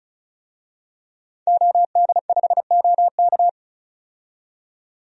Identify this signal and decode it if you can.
Morse code, heard as a keyed tone, about 36 wpm, 700 Hz. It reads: OD5OK
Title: DXCC-100 Call Signs - 5 Characters - 35wpm